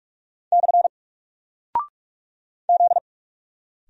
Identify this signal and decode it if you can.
Morse code, heard as a keyed tone, about 44 wpm, 700 Hz. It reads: / Z
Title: Single Character (Letter:Number:Punctuation:Prosign) Alt 45wpm